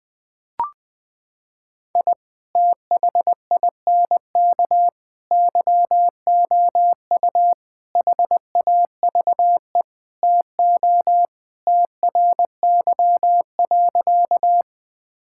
Morse code, heard as a keyed tone, about 21 wpm, 700 Hz. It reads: I THINK YOU HAVE TO TRY.
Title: Sentences with 5-Word Phrases - 20wpm